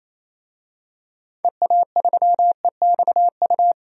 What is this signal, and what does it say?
Morse code, heard as a keyed tone, about 28 wpm, 700 Hz. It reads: EA3EXU
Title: DXCC-100 Call Signs - 6 Characters - Repeated 2x - 28wpm